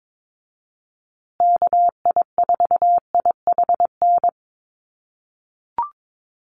Morse code, heard as a keyed tone, about 22 wpm, 700 Hz. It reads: KI4IHN
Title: USA Call Signs Repeated 3x 22wpm